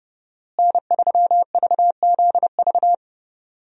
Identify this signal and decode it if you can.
Morse code, heard as a keyed tone, about 32 wpm, 700 Hz. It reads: N3VZV
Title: USA Call Signs - 5 Characters - Repeated 2x - 30wpm